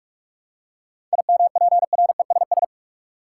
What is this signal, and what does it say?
Morse code, heard as a keyed tone, about 47 wpm, 700 Hz. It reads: IMPRESS
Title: Seven Letter Words - 45wpm